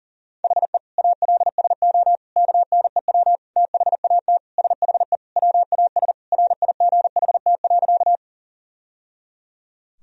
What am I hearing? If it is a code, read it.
Morse code, heard as a keyed tone, about 40 wpm, 700 Hz. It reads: HE ALSO KNEW THAT SHE WAS RIGHT.
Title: Sentences from Top 1000 Words - Review Part 1 - 40wpm